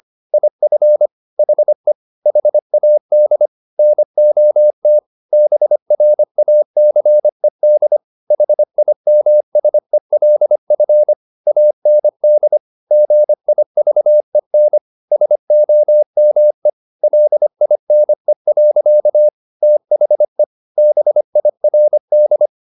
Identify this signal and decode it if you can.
Morse code, heard as a keyed tone, about 25 wpm, 600 Hz. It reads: IF HE HAD NOT BRACED HIMSELF AND GIVEN SOME LINE. THE BIRD
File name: the old man and the sea part 1 25wpm